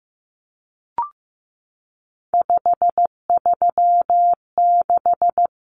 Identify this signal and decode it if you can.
Morse code, heard as a keyed tone, about 15 wpm, 700 Hz. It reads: 536
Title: Numbers - 3-Digit 15wpm